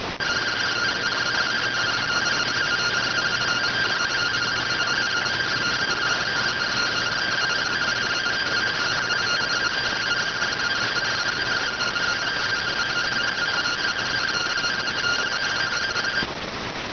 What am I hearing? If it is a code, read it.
Two-tone teleprinter signal, about 170 Hz shift.